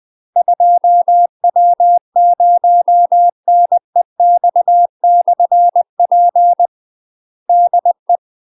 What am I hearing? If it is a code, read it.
Morse code, heard as a keyed tone, about 20 wpm, 700 Hz. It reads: 2W0NEX/P DE
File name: UK MASTER QSOs TRAINING / 20WPM / UK MASTER QSOs 0017